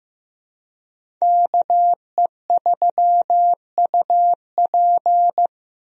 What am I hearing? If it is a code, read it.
Morse code, heard as a keyed tone, about 15 wpm, 700 Hz. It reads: KE3UP
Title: USA Call Signs - 5 Characters - Repeated 3x - 15wpm